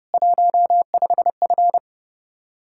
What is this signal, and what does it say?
Morse code, heard as a keyed tone, about 31 wpm, 700 Hz. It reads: 15F